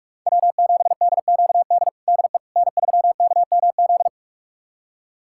Morse code, heard as a keyed tone, about 46 wpm, 700 Hz. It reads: W7DQD DE N3KMZ
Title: USA Callsign DE Callsign - 5 Characters - 45wpm